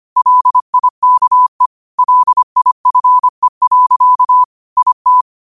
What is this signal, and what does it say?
Morse code, heard as a keyed tone, about 25 wpm, 1000 Hz. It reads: LIKE LIFE. IT